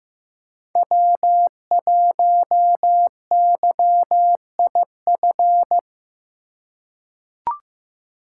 Morse code, heard as a keyed tone, about 15 wpm, 700 Hz. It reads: W1YIF
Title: USA Call Signs - 5 Characters - 3x Character Spacing - 15wpm